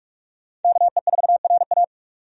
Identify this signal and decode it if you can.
Morse code, heard as a keyed tone, about 46 wpm, 700 Hz. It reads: KE4RA
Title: USA Call Signs - 5 Characters - 45wpm